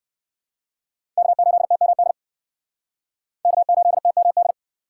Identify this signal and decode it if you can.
Morse code, heard as a keyed tone, about 69 wpm, 700 Hz. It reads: K8TMD K8TMD